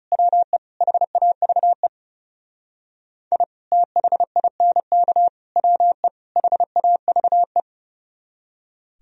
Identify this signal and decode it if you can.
Morse code, heard as a keyed tone, about 39 wpm, 700 Hz. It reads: WE HAVE I THINK WE HAVE